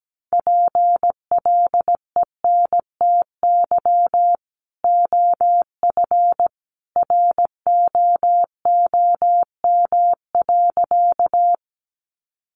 Morse code, heard as a keyed tone, about 17 wpm, 700 Hz. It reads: PLENTY OF ROOM.
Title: Sentences from Top 1300 Words 17wpm